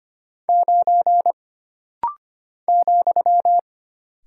Morse code, heard as a keyed tone, about 25 wpm, 700 Hz. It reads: 9 ,
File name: Single Character (Letter:Number:Punctuation:Prosign) - Rapid-Fire - 25wpm